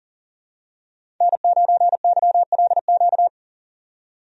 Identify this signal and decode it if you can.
Morse code, heard as a keyed tone, about 40 wpm, 700 Hz. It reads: N9YLQ
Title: USA Call Signs - 5 Characters - 40wpm